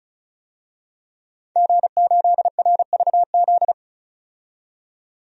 Morse code, heard as a keyed tone, about 34 wpm, 700 Hz. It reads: G8RVZ
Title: DXCC-100 Call Signs - 5 Characters - 35wpm